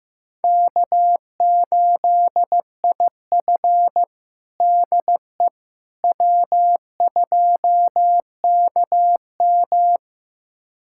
Morse code, heard as a keyed tone, about 15 wpm, 700 Hz. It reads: K8IF DE W2KM
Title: USA Callsign DE Callsign - 4 Characters - 15wpm